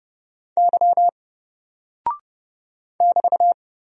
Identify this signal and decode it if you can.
Morse code, heard as a keyed tone, about 30 wpm, 700 Hz. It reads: Y <BT>
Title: Single Character (Letter:Number:Punctuation:Prosign) 30wpm